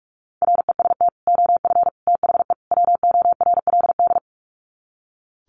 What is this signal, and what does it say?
Morse code, heard as a keyed tone, about 44 wpm, 700 Hz. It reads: REST OF THE WORLD
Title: Sets of 4 Words 45wpm